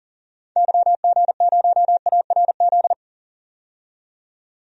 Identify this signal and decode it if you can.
Morse code, heard as a keyed tone, about 37 wpm, 700 Hz. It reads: YG0ARZ
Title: DXCC-100 Call Signs - 6 Characters - 40wpm